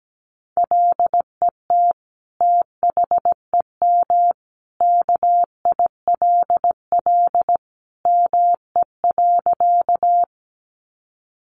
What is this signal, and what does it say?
Morse code, heard as a keyed tone, about 17 wpm, 700 Hz. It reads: LET THEM KILL ME.
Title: Sentences from Top 1400 Words 17wpm